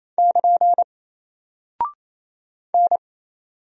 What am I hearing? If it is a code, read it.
Morse code, heard as a keyed tone, about 27 wpm, 700 Hz. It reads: <KN> N
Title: Single Character (Letter:Number:Punctuation:Prosign) Alt 28wpm